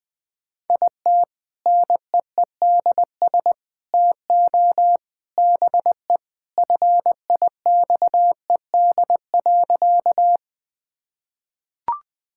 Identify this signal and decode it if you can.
Morse code, heard as a keyed tone, about 21 wpm, 700 Hz. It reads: IT NEEDS TO BE FIXED.
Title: Sentences with 3-Word Phrases - Spoken Beginning - 20wpm